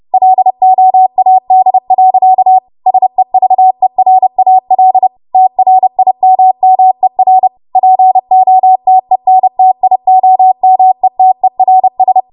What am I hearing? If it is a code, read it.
Morse code, heard as a keyed tone, about 30 wpm, 750 Hz. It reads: LOAD. SEVERAL TRIMMER POTENTIOMETERS
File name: Archive / 30 WPM / 30WPM